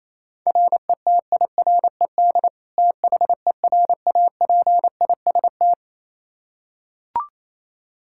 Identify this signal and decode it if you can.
Morse code, heard as a keyed tone, about 28 wpm, 700 Hz. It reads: RETIRED THERAPIST
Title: Practice QSO - Exchange 3 - More Info - 28wpm